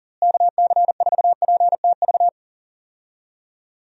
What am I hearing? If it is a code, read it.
Morse code, heard as a keyed tone, about 40 wpm, 700 Hz. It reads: KC4PTV
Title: USA Call Signs - 6 Characters - Repeated 2x - 40wpm